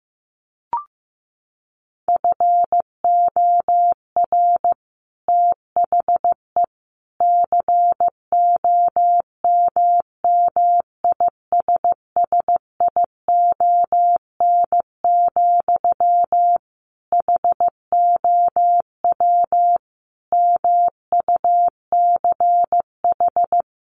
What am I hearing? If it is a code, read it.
Morse code, heard as a keyed tone, about 15 wpm, 700 Hz. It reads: FOR THE COMMISSION, HOW MUCH
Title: Sentences from Top 1400 Words 15wpm